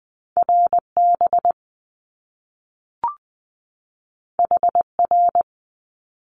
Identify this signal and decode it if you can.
Morse code, heard as a keyed tone, about 20 wpm, 700 Hz. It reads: RB HR
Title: Top Two Letter Combinations in Words 20wpm